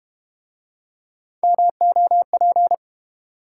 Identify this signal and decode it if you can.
Morse code, heard as a keyed tone, about 32 wpm, 700 Hz. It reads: MOP